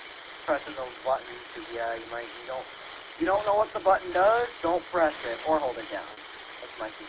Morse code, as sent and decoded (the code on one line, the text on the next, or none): none